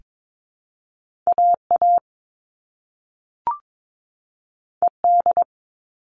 Morse code, heard as a keyed tone, about 22 wpm, 700 Hz. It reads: AA ED
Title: All Cut Numbers - 2-Digit 22wpm